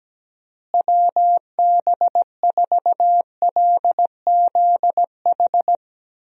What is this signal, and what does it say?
Morse code, heard as a keyed tone, about 17 wpm, 700 Hz. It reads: WB4LZH
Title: USA Call Signs - 6 Characters - Repeated 3x - 17wpm